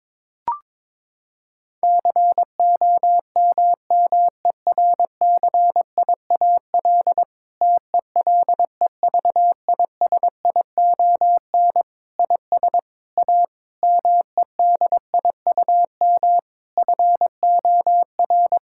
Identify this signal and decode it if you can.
Morse code, heard as a keyed tone, about 22 wpm, 700 Hz. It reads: COMMERCIAL TELEVISION IS A MEDIUM FOR
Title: Sentences from Top 1000 Words - Review Part 6 - 22wpm